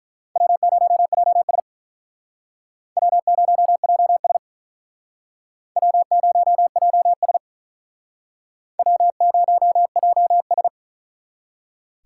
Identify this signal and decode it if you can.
Morse code, heard as a keyed tone, about 48 wpm, 700 Hz. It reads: W0JS W0JS W0JS W0JS